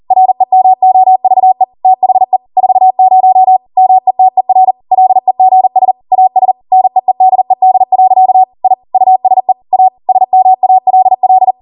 Morse code, heard as a keyed tone, about 40 wpm, 750 Hz. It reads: REMOVE THE 40 METER LEGS AS NEEDED. I USE A SMALL
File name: Morse / 40WPM